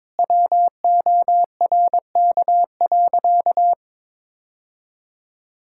Morse code, heard as a keyed tone, about 22 wpm, 700 Hz. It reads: WORK.
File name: Sentences with 7-Word Phrases - 22wpm